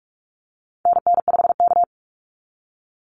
Morse code, heard as a keyed tone, about 44 wpm, 700 Hz. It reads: NN5K
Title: NA SOTA Activator Callsigns 45wpm